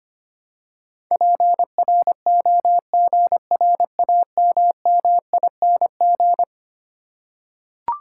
Morse code, heard as a keyed tone, about 25 wpm, 700 Hz. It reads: PROGRAMMING
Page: morse-code-ninja-practice-25wpm